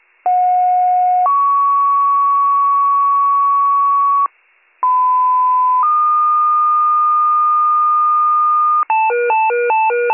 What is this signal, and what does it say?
Two-tone page, heard as a one-second tone, then a three-second tone, about 730 then 1120 Hz, about 1010 then 1220 Hz. It is heard from 0.3 to 8.8 s.